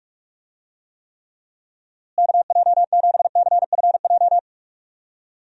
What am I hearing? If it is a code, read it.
Morse code, heard as a keyed tone, about 46 wpm, 700 Hz. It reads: KJ7CFJ
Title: USA Call Signs - 6 Characters - Repeated 2x - 45wpm